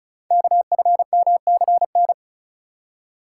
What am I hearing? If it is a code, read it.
Morse code, heard as a keyed tone, about 36 wpm, 690 Hz. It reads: KFMCN
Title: Code Groups - 5 Letters - 35wpm